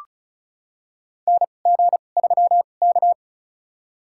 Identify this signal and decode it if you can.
Morse code, heard as a keyed tone, about 36 wpm, spaced as for 26 wpm, 700 Hz. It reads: NG3K
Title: USA Call Signs - 4 Characters - 2x Character Spacing - 35wpm